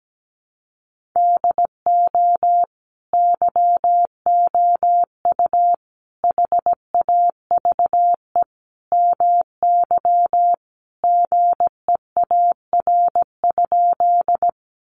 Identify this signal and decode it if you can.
Morse code, heard as a keyed tone, about 17 wpm, 700 Hz. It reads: DO YOU HAVE MY GEAR?
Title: Sentences from Top 1400 Words 17wpm